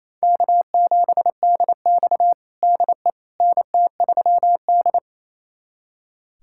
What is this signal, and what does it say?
Morse code, heard as a keyed tone, about 28 wpm, 700 Hz. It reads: K7DX DE NT3D
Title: USA Callsign DE Callsign - 4 Characters - 28wpm